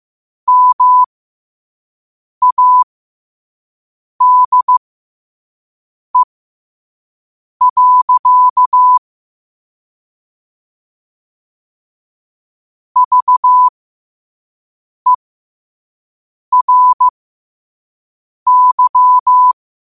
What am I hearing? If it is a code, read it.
Morse code, heard as a keyed tone, about 15 wpm, spaced as for 5 wpm, 1000 Hz. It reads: MADE. VERY
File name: qotdfiles / 5wpm